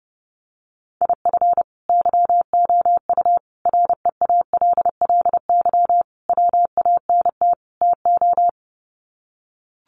Morse code, heard as a keyed tone, about 30 wpm, 700 Hz. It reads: IF YOU REALLY WANT TO
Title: Sets of 5 Words Encore2 30wpm